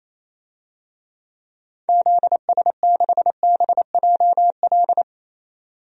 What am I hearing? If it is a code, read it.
Morse code, heard as a keyed tone, about 28 wpm, 700 Hz. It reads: ZS6BJL
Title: DXCC-100 Call Signs - 6 Characters - 28wpm